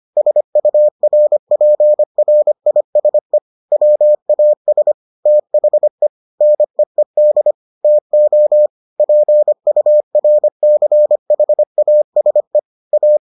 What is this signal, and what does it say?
Morse code, heard as a keyed tone, about 25 wpm, 600 Hz. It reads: SURPRISE WAS THE NEED TO PURCHASE A